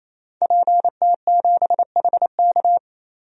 Morse code, heard as a keyed tone, about 28 wpm, 700 Hz. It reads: PT7HK
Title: DXCC-100 Call Signs - 5 Characters - 28wpm